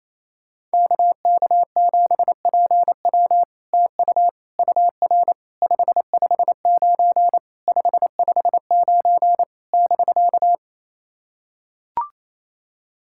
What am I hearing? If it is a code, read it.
Morse code, heard as a keyed tone, about 29 wpm, 700 Hz. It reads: KK7PW TU UR 559 559 <BK>
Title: SOTA QSO Practice 28wpm